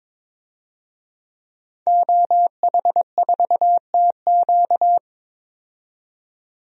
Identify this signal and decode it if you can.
Morse code, heard as a keyed tone, about 22 wpm, 700 Hz. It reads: OH4TQ
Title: DXCC-100 Call Signs - 5 Characters - 22wpm